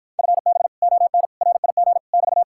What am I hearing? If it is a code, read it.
Morse code, heard as a keyed tone, about 52 wpm, 700 Hz. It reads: FB ON RIG <BT>